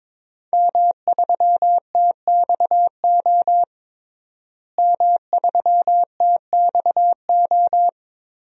Morse code, heard as a keyed tone, about 22 wpm, 700 Hz. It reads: M3TXO M3TXO
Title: DXCC-100 Call Signs - 5 Characters - Repeated 2x - 22wpm